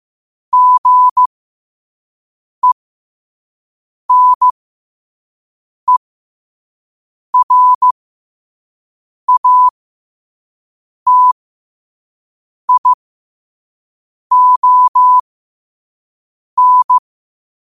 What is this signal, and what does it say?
Morse code, heard as a keyed tone, about 15 wpm, spaced as for 5 wpm, 1000 Hz. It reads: GENERATION
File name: 5wpm